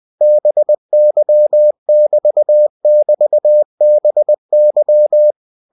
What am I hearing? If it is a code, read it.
Morse code, heard as a keyed tone, about 20 wpm, 600 Hz. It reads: BY <BT><BT> BY
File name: cw-books / winnie-the-pooh / winnie-20wpm-page-0091